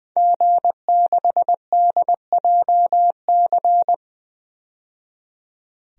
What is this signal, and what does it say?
Morse code, heard as a keyed tone, about 20 wpm, 700 Hz. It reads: G6DJC
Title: DXCC-100 Call Signs - 5 Characters - 20wpm